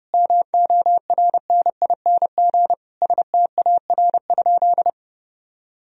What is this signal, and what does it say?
Morse code, heard as a keyed tone, about 30 wpm, 700 Hz. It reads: MORNING STAR?
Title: Sentences from Top 1000 Words - Review Part 2 - 30wpm